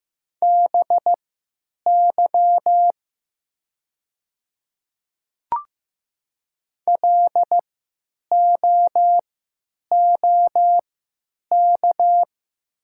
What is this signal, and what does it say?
Morse code, heard as a keyed tone, about 15 wpm, spaced as for 9 wpm, 700 Hz. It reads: BY LOOK
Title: 3x Character Spacing - Top 100 Words - 15wpm